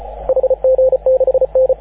Morse code, heard as a keyed tone, about 34 wpm, 530 Hz, with radio noise. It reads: HG6N
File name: AUDIO2019 / HG6N CW SSB MIX